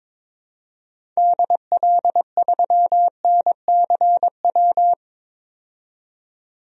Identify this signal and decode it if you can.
Morse code, heard as a keyed tone, about 22 wpm, 700 Hz. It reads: DL3NCW
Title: DXCC-100 Call Signs - 6 Characters - 22wpm